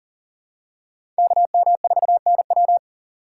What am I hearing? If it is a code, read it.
Morse code, heard as a keyed tone, about 40 wpm, 700 Hz. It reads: KM4NW
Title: USA Call Signs - 5 Characters - 3x Character Spacing - 40wpm